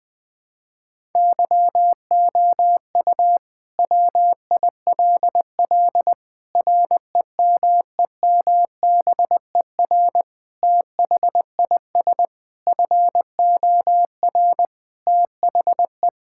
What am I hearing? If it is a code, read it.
Morse code, heard as a keyed tone, about 20 wpm, 700 Hz. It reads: YOU WILL REMEMBER THIS FOR THE
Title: Sentences with 5-Word Phrases - Spoken Beginning - 20wpm